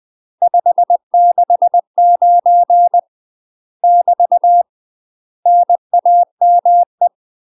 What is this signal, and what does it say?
Morse code, heard as a keyed tone, about 20 wpm, 700 Hz. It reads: 569 <BT> NAME
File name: UK MASTER QSOs TRAINING / 20WPM / UK MASTER QSOs 0072